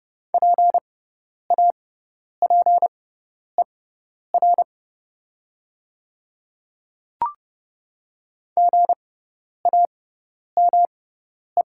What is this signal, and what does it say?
Morse code, heard as a keyed tone, about 29 wpm, spaced as for 10 wpm, 700 Hz. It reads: PAPER GAME
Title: 6x Character Spacing - Top 500 Words - 30wpm